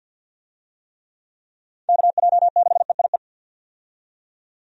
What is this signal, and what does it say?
Morse code, heard as a keyed tone, about 50 wpm, 700 Hz. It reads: KJ6EIE
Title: USA Call Signs - 6 Characters - 50wpm